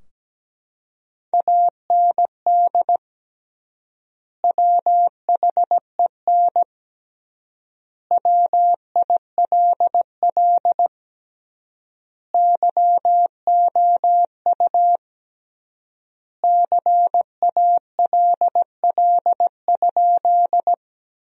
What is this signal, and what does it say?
Morse code, heard as a keyed tone, about 17 wpm, 700 Hz. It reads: AND WHEN WILL YOU CALL?
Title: 3x Word-Spacing - Sentences from Top 100 Words - 17wpm